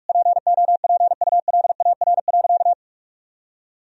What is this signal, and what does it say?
Morse code, heard as a keyed tone, about 46 wpm, 700 Hz. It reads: POPULAR.
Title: Sentences from Top 1000 Words - Review Part 3 - 45wpm